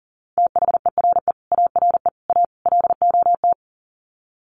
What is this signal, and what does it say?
Morse code, heard as a keyed tone, about 40 wpm, 700 Hz. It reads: THERE ARE A LOT
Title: Sets of 4 Words 40wpm